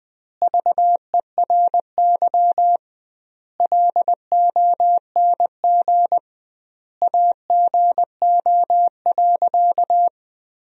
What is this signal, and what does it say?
Morse code, heard as a keyed tone, about 20 wpm, 700 Hz. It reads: VERY LONG AGO.
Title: 2x Word-Spacing - Sentences from Top 500 Words - 20wpm